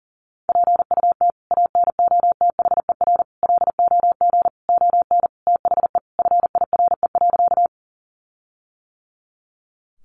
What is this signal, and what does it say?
Morse code, heard as a keyed tone, about 40 wpm, 700 Hz. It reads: PUT ANOTHER LOG ON THE FIRE.
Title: Sentences from Top 1400 Words 40wpm